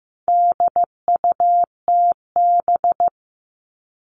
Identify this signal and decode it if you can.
Morse code, heard as a keyed tone, about 15 wpm, 700 Hz. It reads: DUTB
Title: All Cut Numbers - 4-Digit 15wpm